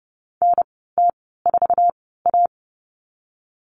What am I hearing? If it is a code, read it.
Morse code, heard as a keyed tone, about 30 wpm, spaced as for 17 wpm, 700 Hz. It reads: NT4A
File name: Lesson 19g - Letter w - Call Signs 30wpm